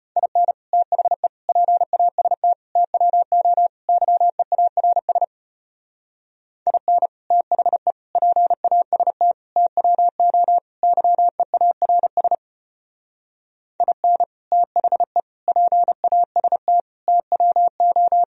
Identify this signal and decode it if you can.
Morse code, heard as a keyed tone, about 33 wpm, 700 Hz. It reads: IN THE PAST TWO YEARS IN THE PAST TWO YEARS IN THE PAST TWO